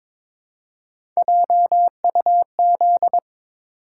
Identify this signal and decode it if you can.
Morse code, heard as a keyed tone, about 22 wpm, 700 Hz. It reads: JUZ
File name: practice / 22wpm / Three Letter-Number Combinations - 22wpm